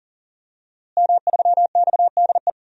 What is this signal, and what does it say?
Morse code, heard as a keyed tone, about 37 wpm, 700 Hz. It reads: M3XDE
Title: DXCC-100 Call Signs - 5 Characters - Repeated 2x - 40wpm